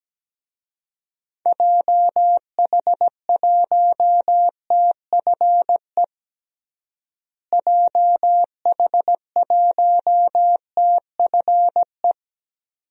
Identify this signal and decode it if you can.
Morse code, heard as a keyed tone, about 17 wpm, 700 Hz. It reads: JH1TFE JH1TFE
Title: DXCC-100 Call Signs - 6 Characters - Repeated 2x - 17wpm